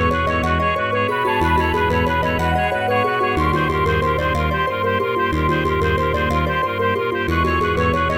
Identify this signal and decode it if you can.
There is no code.